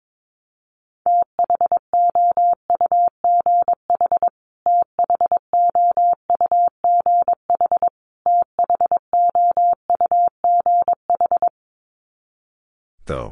Morse code, heard as a keyed tone, about 22 wpm, 700 Hz. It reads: THOUGH THOUGH THOUGH
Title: Top 500 Words Repeated 22wpm